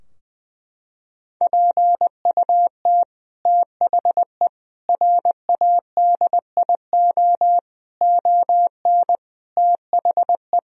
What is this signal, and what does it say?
Morse code, heard as a keyed tone, about 20 wpm, 700 Hz. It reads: PUT THE RADIO ON THE
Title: Sentences from Top 1000 Words - Review Part 3 - 20wpm